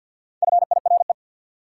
Morse code, heard as a keyed tone, about 51 wpm, 700 Hz. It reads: FIRE
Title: Sentences from Top 400 Words - New Word Drill - 50wpm